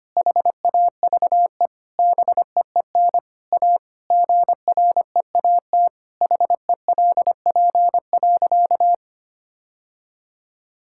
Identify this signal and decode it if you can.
Morse code, heard as a keyed tone, about 25 wpm, 700 Hz. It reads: HAVE BEEN A GREAT HELP.
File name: practice / 25wpm / Sentences with 3-Word Phrases - 25wpm